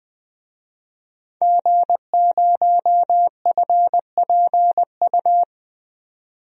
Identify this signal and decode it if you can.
Morse code, heard as a keyed tone, about 20 wpm, 700 Hz. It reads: G0FPU
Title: DXCC-100 Call Signs - 5 Characters - 20wpm